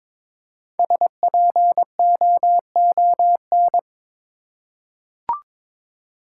Morse code, heard as a keyed tone, about 22 wpm, 700 Hz. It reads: SPOON
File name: 5 Letter Words - First 3 Letters Given - 22wpm